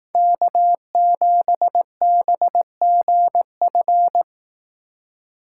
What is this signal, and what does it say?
Morse code, heard as a keyed tone, about 19 wpm, 700 Hz. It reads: K7BGF